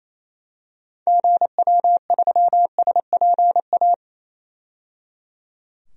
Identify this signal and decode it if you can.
Morse code, heard as a keyed tone, about 28 wpm, 700 Hz. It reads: GW3SPA
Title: DXCC-100 Call Signs - 6 Characters - 28wpm